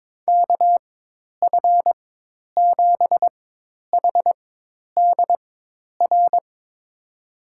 Morse code, heard as a keyed tone, about 22 wpm, spaced as for 10 wpm, 700 Hz. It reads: KF7HDR
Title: USA Call Signs - 6 Characters - 4x Character Spacing - 22wpm